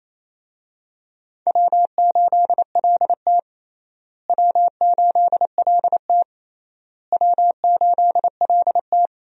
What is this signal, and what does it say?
Morse code, heard as a keyed tone, about 28 wpm, 700 Hz. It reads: W8LT W8LT W8LT
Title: USA Call Signs - 4 Characters - Repeated 3x - 28wpm